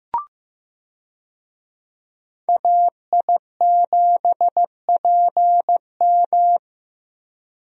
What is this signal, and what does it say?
Morse code, heard as a keyed tone, about 16 wpm, 700 Hz. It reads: AI7PM
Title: USA Call Signs - 5 Characters - Repeated 3x - 15wpm